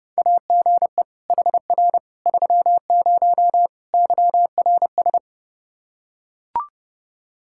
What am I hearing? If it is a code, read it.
Morse code, heard as a keyed tone, about 32 wpm, 700 Hz. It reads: AGE HR 30 YRS
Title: Practice QSO - Exchange 3 - More Info - 30wpm